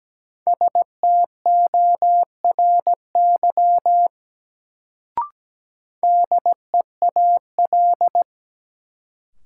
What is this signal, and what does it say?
Morse code, heard as a keyed tone, about 17 wpm, 700 Hz. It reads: STORY DEAL
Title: Sets of 3 Words - New Word Drill - 17wpm